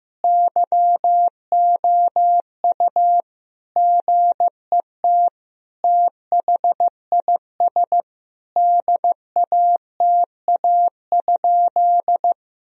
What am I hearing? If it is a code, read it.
Morse code, heard as a keyed tone, about 15 wpm, 700 Hz. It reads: YOU GET THIS DATA?
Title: Sentences from Top 1000 Words - Review Part 3 - 15wpm